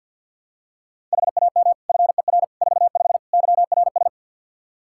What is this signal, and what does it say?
Morse code, heard as a keyed tone, about 52 wpm, 700 Hz. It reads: HAM FER 45 YRS
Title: Practice QSO - Exchange 3 - More Info - 50wpm